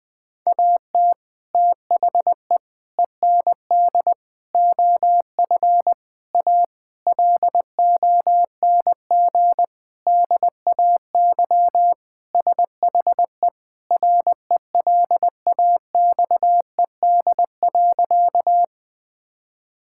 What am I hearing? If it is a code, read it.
Morse code, heard as a keyed tone, about 19 wpm, 700 Hz. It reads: AT THE END OF A LONG DAY SHE RELAXED.
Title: Sentences with 4-Word Phrases - Spoken Beginning - 20wpm